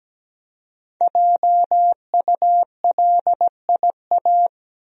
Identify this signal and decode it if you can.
Morse code, heard as a keyed tone, about 17 wpm, 700 Hz. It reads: JULIA
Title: USA Common Names - 17wpm